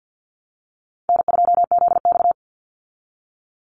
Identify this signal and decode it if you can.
Morse code, heard as a keyed tone, about 50 wpm, 700 Hz. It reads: N2ZX
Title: Easy Call Signs Repeated 2x - 50wpm